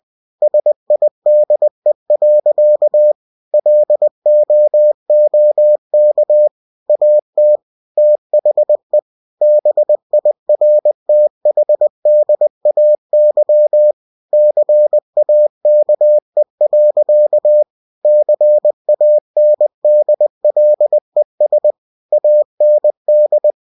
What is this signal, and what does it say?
Morse code, heard as a keyed tone, about 20 wpm, 600 Hz. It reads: SIDE. LOOK AT THE BIRTHDAY CAKE. CANDLES AND
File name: cw-books / winnie-the-pooh / winnie-20wpm-page-0047